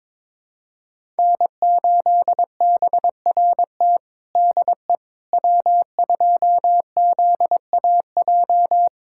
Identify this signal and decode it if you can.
Morse code, heard as a keyed tone, about 22 wpm, 700 Hz. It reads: N8BRT DE W2ZAJ
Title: USA Callsign DE Callsign - 5 Characters - 22wpm